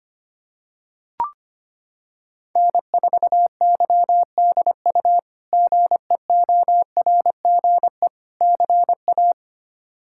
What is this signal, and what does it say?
Morse code, heard as a keyed tone, about 25 wpm, 700 Hz. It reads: N4YDU GEORGE CA
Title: K1USN SST QSO Practice - 25wpm